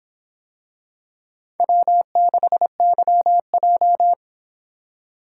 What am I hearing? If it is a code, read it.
Morse code, heard as a keyed tone, about 25 wpm, 700 Hz. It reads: W6YJ